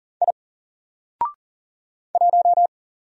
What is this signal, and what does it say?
Morse code, heard as a keyed tone, about 38 wpm, 700 Hz. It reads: I 1
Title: Single Character (Letter:Number:Punctuation:Prosign) 40wpm